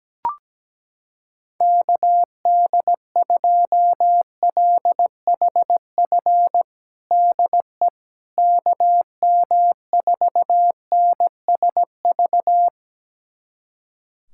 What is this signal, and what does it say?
Morse code, heard as a keyed tone, about 17 wpm, 700 Hz. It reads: KD2LHF DE KM4NSV
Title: USA Callsign DE Callsign - 6 Characters - 17wpm